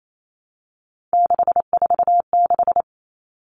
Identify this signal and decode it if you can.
Morse code, heard as a keyed tone, about 28 wpm, 700 Hz. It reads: 646
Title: Three Challenging Letter-Number Combinations 28wpm